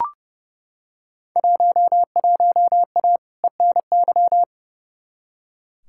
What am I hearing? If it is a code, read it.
Morse code, heard as a keyed tone, about 30 wpm, 700 Hz. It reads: 11A ENY
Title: Field Day Exchanges Encore - 30wpm